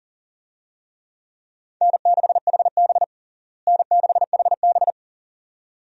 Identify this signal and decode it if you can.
Morse code, heard as a keyed tone, about 40 wpm, 700 Hz. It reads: N6HB N6HB
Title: USA Call Signs - 4 Characters - Repeated 2x - 40wpm